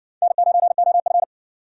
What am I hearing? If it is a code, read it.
Morse code, heard as a keyed tone, about 60 wpm, 700 Hz. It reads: N9OF